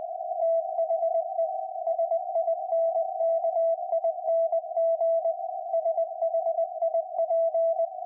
Morse code, heard as a keyed tone, about 20 wpm, 660 Hz. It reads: THE SINKING SHIP
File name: IC-7100